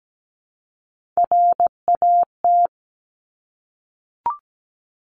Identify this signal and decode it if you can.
Morse code, heard as a keyed tone, about 17 wpm, 700 Hz. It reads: RAT
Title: Top Three Letter Combinations in Words 17wpm